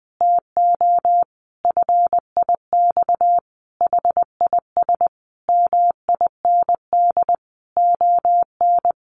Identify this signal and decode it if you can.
Morse code, heard as a keyed tone, about 19 wpm, 700 Hz. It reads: TO FIX HIS MIND ON
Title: Sentences from Top 1400 Words 20wpm